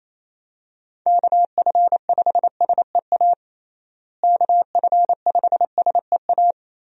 Morse code, heard as a keyed tone, about 27 wpm, 700 Hz. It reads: KF5SEA KF5SEA
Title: USA Call Signs - 6 Characters - Repeated 3x - 28wpm